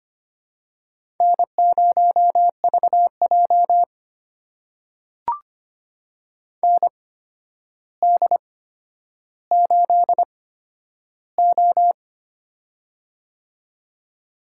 Morse code, heard as a keyed tone, about 25 wpm, spaced as for 11 wpm, 700 Hz. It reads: N0VJ ND8O
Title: USA Call Signs - 4 Characters - 8x Character Spacing - 25wpm